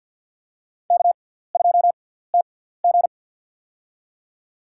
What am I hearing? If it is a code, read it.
Morse code, heard as a keyed tone, about 47 wpm, spaced as for 17 wpm, 700 Hz. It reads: K2TG
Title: USA Call Signs - 4 Characters - 6x Character Spacing - 50wpm